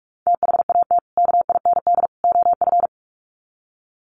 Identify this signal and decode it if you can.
Morse code, heard as a keyed tone, about 44 wpm, 700 Hz. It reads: THAT KIND OF
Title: Sets of 3 Words 45wpm